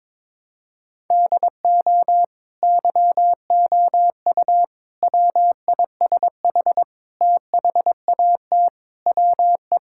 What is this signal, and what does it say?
Morse code, heard as a keyed tone, about 22 wpm, 700 Hz. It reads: DO YOU WISH THAT WE
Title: Sentences from Top 1600 Words 22wpm